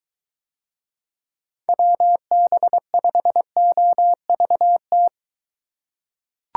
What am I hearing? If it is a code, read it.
Morse code, heard as a keyed tone, about 23 wpm, 700 Hz. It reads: WB5OVT